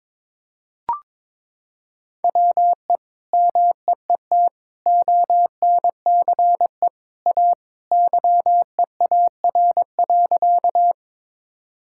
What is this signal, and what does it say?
Morse code, heard as a keyed tone, about 22 wpm, 700 Hz. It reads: WE MEET ONCE A YEAR.
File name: practice / 22wpm / Sentences with 2-Word Phrases - 22wpm